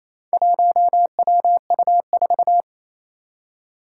Morse code, heard as a keyed tone, about 28 wpm, 700 Hz. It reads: 1WU4
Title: Four Challenging Letter-Number Combinations 28wpm